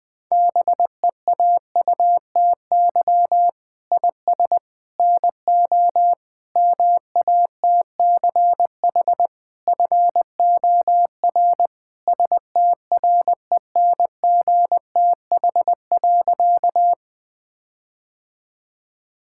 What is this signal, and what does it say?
Morse code, heard as a keyed tone, about 20 wpm, 700 Hz. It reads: BEAUTY IS NO MATCH FOR STRENGTH.
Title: Sentences from Top 1600 Words 20wpm